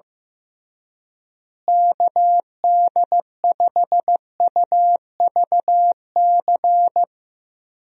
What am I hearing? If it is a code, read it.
Morse code, heard as a keyed tone, about 16 wpm, 700 Hz. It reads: KD5UVC
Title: USA Call Signs - 6 Characters - Repeated 3x - 15wpm